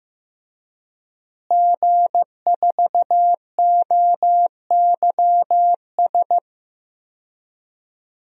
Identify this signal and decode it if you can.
Morse code, heard as a keyed tone, about 15 wpm, 700 Hz. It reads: G4OYS
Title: DXCC-100 Call Signs - 5 Characters - Repeated 2x - 15wpm